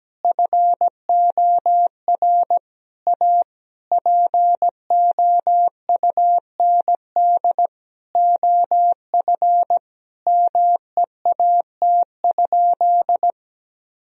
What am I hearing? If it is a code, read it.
Morse code, heard as a keyed tone, about 17 wpm, 700 Hz. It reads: FOR A POUND OF MEAT?
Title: Sentences from Top 1000 Words - Review Part 3 - 17wpm